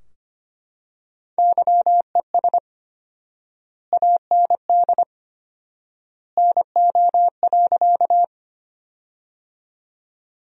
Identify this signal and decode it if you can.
Morse code, heard as a keyed tone, about 25 wpm, 700 Hz. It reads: YES AND NO.
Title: 4x Word-Spacing - Sentences from Top 500 Words - 25wpm